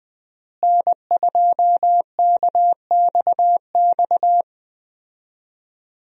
Morse code, heard as a keyed tone, about 21 wpm, 700 Hz. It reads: N2KXX
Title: USA Call Signs - 5 Characters - Repeated 3x - 20wpm